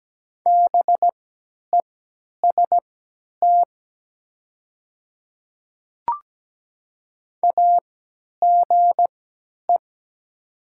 Morse code, heard as a keyed tone, about 17 wpm, spaced as for 10 wpm, 690 Hz. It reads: BEST AGE
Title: 3x Character Spacing - Top 500 Words - 17wpm